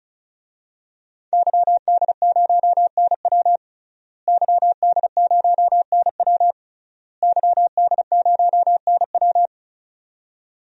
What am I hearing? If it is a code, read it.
Morse code, heard as a keyed tone, about 36 wpm, 700 Hz. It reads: YD0NW YD0NW YD0NW
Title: DXCC-100 Call Signs - 5 Characters - Repeated 3x - 35wpm